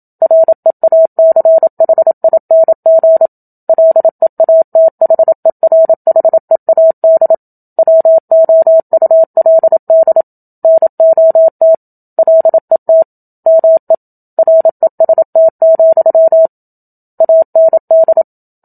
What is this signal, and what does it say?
Morse code, heard as a keyed tone, about 27 wpm, 650 Hz. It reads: REACHING LEATHERHEAD WOULD NOT LET ME REST, AND
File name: Book Two / 01-Chapter 1 - 27 WPM